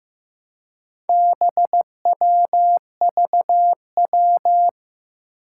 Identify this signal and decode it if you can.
Morse code, heard as a keyed tone, about 15 wpm, 700 Hz. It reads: BWVW
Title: Four Challenging Letter-Number Combinations 15wpm